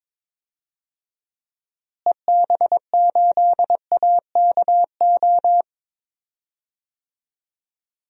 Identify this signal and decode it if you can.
Morse code, heard as a keyed tone, about 22 wpm, 700 Hz. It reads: EB8AKO
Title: DXCC-100 Call Signs - 6 Characters - Repeated 3x - 22wpm